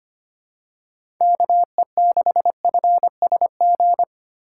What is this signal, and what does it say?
Morse code, heard as a keyed tone, about 25 wpm, 700 Hz. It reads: KE6FSG
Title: USA Call Signs - 6 Characters - 4x Character Spacing - 25wpm